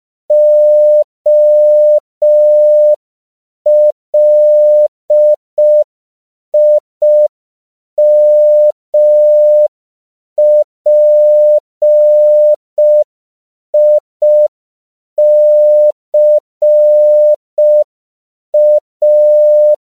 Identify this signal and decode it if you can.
Morse code, heard as a keyed tone, about 5 wpm, 600 Hz. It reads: OLIMPICA
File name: CW / 5PPM